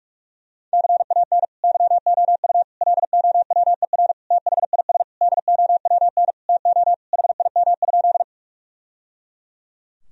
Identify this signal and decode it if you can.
Morse code, heard as a keyed tone, about 41 wpm, 700 Hz. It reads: CAN YOU LOWER THIS DOWN TO HIM?
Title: Sentences from Top 1000 Words - Review Part 4 - 45wpm